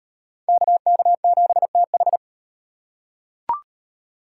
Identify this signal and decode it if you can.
Morse code, heard as a keyed tone, about 37 wpm, 700 Hz. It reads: KK7TH